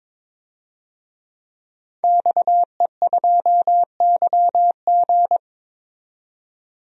Morse code, heard as a keyed tone, about 22 wpm, 700 Hz. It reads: XE2YG
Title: DXCC-100 Call Signs - 5 Characters - 22wpm